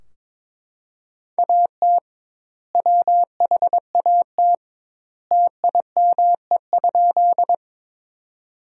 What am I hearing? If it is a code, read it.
Morse code, heard as a keyed tone, about 22 wpm, 700 Hz. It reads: AT WHAT TIME?
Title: 2x Word-Spacing - Sentences from Top 100 Words - 22wpm